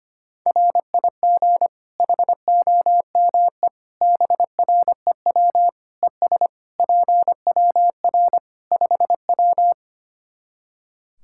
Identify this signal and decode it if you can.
Morse code, heard as a keyed tone, about 25 wpm, 700 Hz. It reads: RIG HOME BREW ES PWR 5W
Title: Practice QSO - Exchange 2 - Info - 25wpm